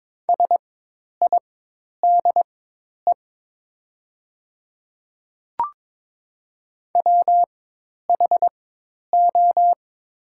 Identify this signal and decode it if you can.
Morse code, heard as a keyed tone, about 21 wpm, spaced as for 10 wpm, 700 Hz. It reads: SIDE WHO
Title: 4x Character Spacing - Top 100 Words - 22wpm